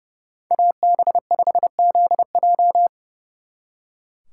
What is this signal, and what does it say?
Morse code, heard as a keyed tone, about 37 wpm, 700 Hz. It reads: AB5ZJ
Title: Speed Racing - USA 2x2 Call Signs 30wpm